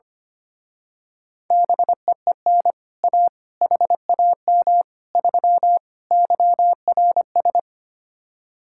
Morse code, heard as a keyed tone, about 25 wpm, 700 Hz. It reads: BEEN A HAM 3 YRS
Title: Speed-Racing Common QSO Phrases Encore - 25wpm